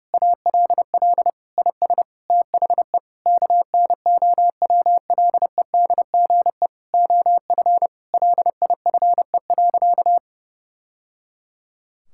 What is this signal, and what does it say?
Morse code, heard as a keyed tone, about 32 wpm, 700 Hz. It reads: ALL IS THE KNOWLEDGE OF LIFE.
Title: Sentences from Top 1000 Words - Review Part 6 - 30wpm